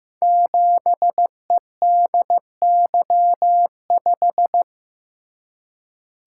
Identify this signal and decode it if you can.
Morse code, heard as a keyed tone, about 15 wpm, 700 Hz. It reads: 7EDY5
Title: Code Groups - 5 Letter:Numbers - 15wpm